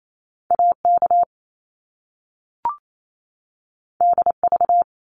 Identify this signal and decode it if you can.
Morse code, heard as a keyed tone, about 28 wpm, 700 Hz. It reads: AK DV
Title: Top Two Letter Combinations in Words 28wpm